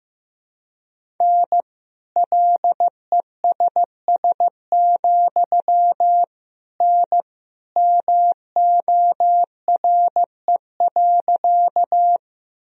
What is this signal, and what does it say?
Morse code, heard as a keyed tone, about 15 wpm, 700 Hz. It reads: N LESS, N MORE.
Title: Sentences from Top 400 Words and Abbreviations 15wpm